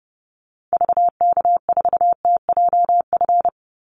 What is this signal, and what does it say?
Morse code, heard as a keyed tone, about 30 wpm, 700 Hz. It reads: VK4TJF
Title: Call Signs Repeated 3x - 30wpm